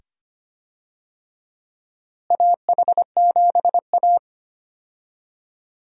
Morse code, heard as a keyed tone, about 25 wpm, 700 Hz. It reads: AH7A
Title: USA Call Signs - 4 Characters - Repeated 2x - 25wpm